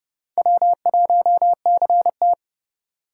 Morse code, heard as a keyed tone, about 40 wpm, spaced as for 35 wpm, 700 Hz. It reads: W1CT
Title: Speed Racing - USA 1x2 Call Signs 30wpm